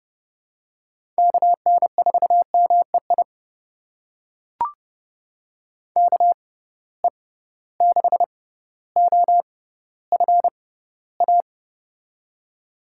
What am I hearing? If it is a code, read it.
Morse code, heard as a keyed tone, about 30 wpm, spaced as for 15 wpm, 700 Hz. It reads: KN4MEI KE6OFA
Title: USA Call Signs - 6 Characters - 6x Character Spacing - 30wpm